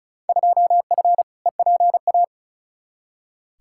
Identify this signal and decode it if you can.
Morse code, heard as a keyed tone, about 34 wpm, 700 Hz. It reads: 2F EPA
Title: Field Day Exchanges - 35wpm